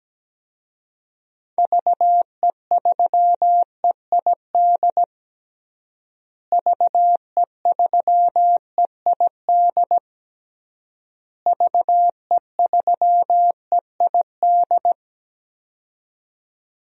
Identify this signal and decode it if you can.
Morse code, heard as a keyed tone, about 17 wpm, 700 Hz. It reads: VE3EID VE3EID VE3EID
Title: DXCC-100 Call Signs - 6 Characters - Repeated 3x - 17wpm